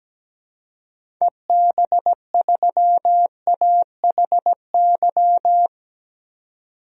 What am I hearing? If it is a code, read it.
Morse code, heard as a keyed tone, about 17 wpm, 700 Hz. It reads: EB3AHY
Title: DXCC-100 Call Signs - 6 Characters - 17wpm